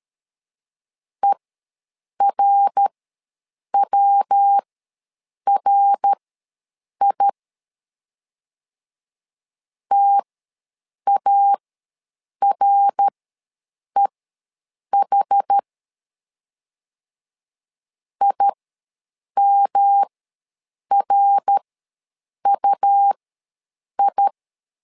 Morse code, heard as a keyed tone, about 13 wpm, spaced as for 7 wpm, 800 Hz. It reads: ERWRI TAREH IMRUI
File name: LEZIONI / 3 MWRO / 3 MWR ws 12wpm 1809